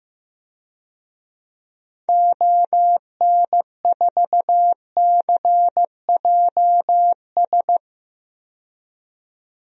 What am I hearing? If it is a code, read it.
Morse code, heard as a keyed tone, about 15 wpm, 700 Hz. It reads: ON4CJS
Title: DXCC-100 Call Signs - 6 Characters - 15wpm